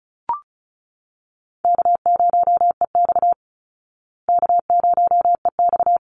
Morse code, heard as a keyed tone, about 34 wpm, 700 Hz. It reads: K0EX K0EX
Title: Easy Call Signs Repeated 2x - 35wpm